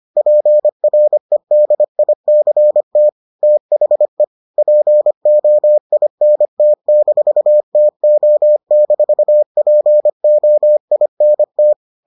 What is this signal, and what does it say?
Morse code, heard as a keyed tone, about 25 wpm, 600 Hz. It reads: PREDICT THE POINT-TO-POINT